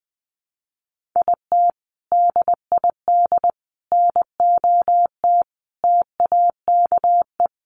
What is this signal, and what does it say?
Morse code, heard as a keyed tone, about 20 wpm, 700 Hz. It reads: IT DID NOT TAKE
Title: Sets of 5 Words Encore2 20wpm